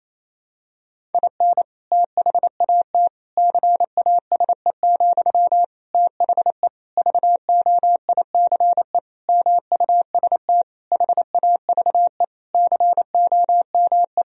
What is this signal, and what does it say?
Morse code, heard as a keyed tone, about 28 wpm, 700 Hz. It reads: IN THAT CASE, THE VOICE MUST HAVE COME
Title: Sentences from Top 1000 Words - Review Part 2 - 28wpm